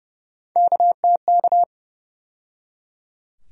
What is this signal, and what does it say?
Morse code, heard as a keyed tone, about 31 wpm, 700 Hz. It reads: KTK
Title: USA Call Sign Postfixes - 3 Letters 30wpm